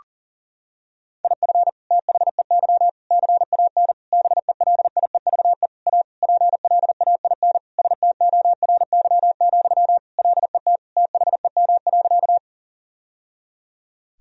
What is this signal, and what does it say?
Morse code, heard as a keyed tone, about 40 wpm, 700 Hz. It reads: IF THEY CAN BELIEVE A PLAIN STORY, LET THEM.
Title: Sentences from Top 1000 Words - Review Part 2 - 40wpm